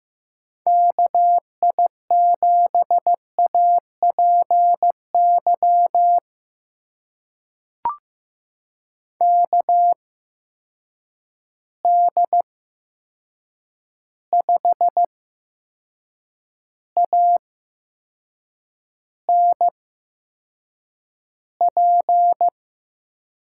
Morse code, heard as a keyed tone, about 15 wpm, spaced as for 6 wpm, 700 Hz. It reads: KI7APY KD5ANP
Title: USA Call Signs - 6 Characters - 8x Character Spacing - 15wpm